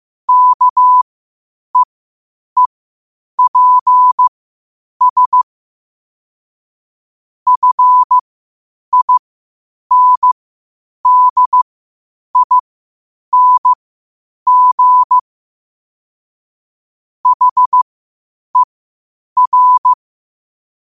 Morse code, heard as a keyed tone, about 15 wpm, spaced as for 8 wpm, 1000 Hz. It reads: KEEPS FINDING HER
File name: qotdfiles / 8wpm